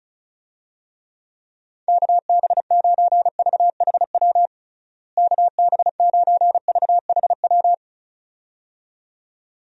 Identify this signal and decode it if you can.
Morse code, heard as a keyed tone, about 34 wpm, 700 Hz. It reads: KB9VHW KB9VHW
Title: USA Call Signs - 6 Characters - Repeated 2x - 35wpm